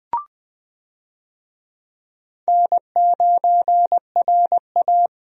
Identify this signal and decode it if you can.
Morse code, heard as a keyed tone, about 20 wpm, 700 Hz. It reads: N9RA
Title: USA Call Signs - 4 Characters - 20wpm